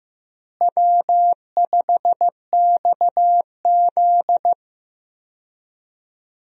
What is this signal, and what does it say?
Morse code, heard as a keyed tone, about 15 wpm, 700 Hz. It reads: W5XZ
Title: USA Call Signs - 4 Characters - Repeated 2x - 15wpm